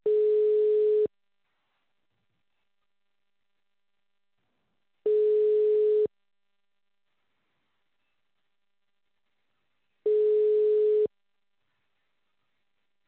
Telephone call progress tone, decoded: ringback tone